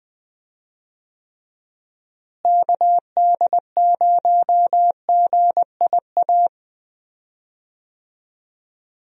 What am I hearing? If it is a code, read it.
Morse code, heard as a keyed tone, about 20 wpm, 700 Hz. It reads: KD0GIA
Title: USA Call Signs - 6 Characters - Repeated 3x - 20wpm